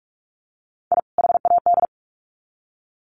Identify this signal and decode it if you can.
Morse code, heard as a keyed tone, about 44 wpm, 700 Hz. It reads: I HAD
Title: Sets of 2 Words 45wpm